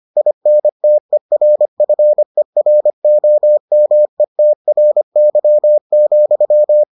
Morse code, heard as a keyed tone, about 25 wpm, 600 Hz. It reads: INTERFEROMETRY,